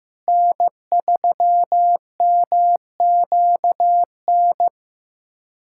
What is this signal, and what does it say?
Morse code, heard as a keyed tone, about 15 wpm, 700 Hz. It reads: N3MQN
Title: USA Callsign DE Callsign 15wpm